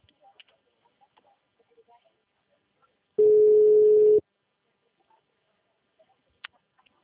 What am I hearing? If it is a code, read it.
Ringback tone (Europe, 425 Hz).
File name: out- -2012- - - .12805